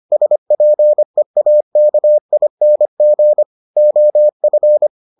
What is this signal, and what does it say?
Morse code, heard as a keyed tone, about 25 wpm, 600 Hz. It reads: SPEAKING OF